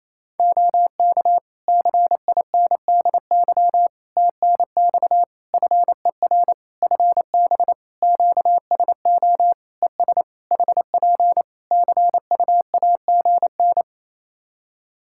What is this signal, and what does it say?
Morse code, heard as a keyed tone, about 27 wpm, 700 Hz. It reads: OK CINDY TNX FER FB QSO ES HP CUAGN
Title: K1USN SST QSO Practice 28wpm